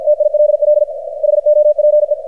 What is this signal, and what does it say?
Morse code, heard as a keyed tone, about 51 wpm, 600 Hz, with radio noise. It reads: TILL HOPE
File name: archived / CRSnet